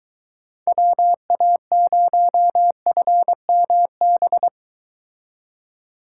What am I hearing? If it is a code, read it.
Morse code, heard as a keyed tone, about 23 wpm, 700 Hz. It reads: WA0FMB